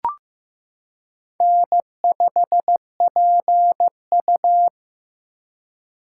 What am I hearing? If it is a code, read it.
Morse code, heard as a keyed tone, about 15 wpm, 700 Hz. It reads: N5PU
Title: K1USN SST QSO Practice - 15wpm